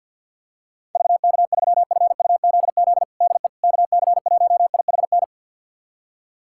Morse code, heard as a keyed tone, about 46 wpm, 700 Hz. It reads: VK3FUZZ DE KC1IHN
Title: USA Callsign DE Callsign 50wpm